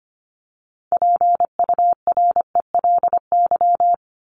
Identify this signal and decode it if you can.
Morse code, heard as a keyed tone, about 25 wpm, 700 Hz. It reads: PURELY
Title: LY Suffix - Top 100 Words with Base Words 25wpm